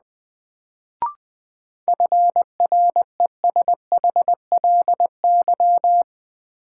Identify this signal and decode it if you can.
Morse code, heard as a keyed tone, about 21 wpm, 700 Hz. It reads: FRESHLY
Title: Sets of 5 Words Encore - New Word Drill - 20wpm